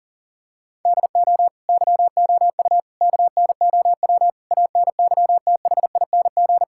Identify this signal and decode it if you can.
Morse code, heard as a keyed tone, about 37 wpm, 700 Hz. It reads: DO YOU KNOW ANYTHING
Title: Sentences from Top 1600 Words 40wpm